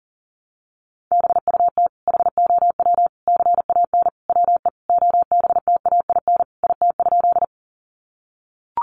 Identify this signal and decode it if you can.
Morse code, heard as a keyed tone, about 40 wpm, 700 Hz. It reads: BUT HOW CAN WE OBTAIN IT?
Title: Sentences from Top 1300 Words 40wpm